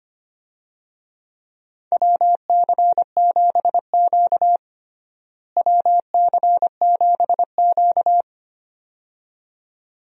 Morse code, heard as a keyed tone, about 25 wpm, 700 Hz. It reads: WC7Q WC7Q
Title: USA Call Signs - 4 Characters - Repeated 2x - 25wpm